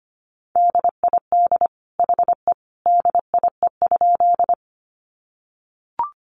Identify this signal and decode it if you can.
Morse code, heard as a keyed tone, about 25 wpm, 700 Hz. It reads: DID HE DIE?
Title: Sentences from Top 1300 Words 25wpm